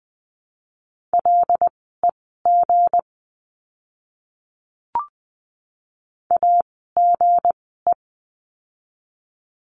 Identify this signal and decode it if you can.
Morse code, heard as a keyed tone, about 20 wpm, spaced as for 15 wpm, 700 Hz. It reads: LEG AGE
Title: 160 Lesson 26f - Letter g - Words 20wpm